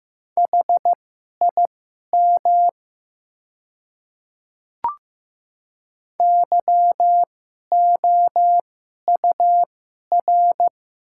Morse code, heard as a keyed tone, about 15 wpm, spaced as for 11 wpm, 700 Hz. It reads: HIM YOUR
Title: 2x Character Spacing - Top 100 Words - 15wpm